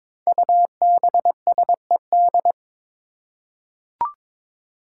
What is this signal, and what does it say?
Morse code, heard as a keyed tone, about 22 wpm, 700 Hz. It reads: UBSED
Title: Five Challenging Letter-Number Combinations 22wpm